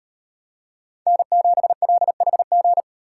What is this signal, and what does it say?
Morse code, heard as a keyed tone, about 38 wpm, 700 Hz. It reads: N7LHG